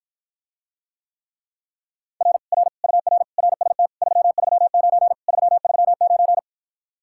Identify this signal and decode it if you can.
Morse code, heard as a keyed tone, about 54 wpm, 700 Hz. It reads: R R UR RST 339 339